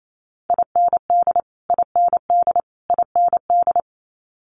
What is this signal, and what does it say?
Morse code, heard as a keyed tone, about 28 wpm, 700 Hz. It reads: IND IND IND
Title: Multi-letter Phonemes Repeated 3x 28wpm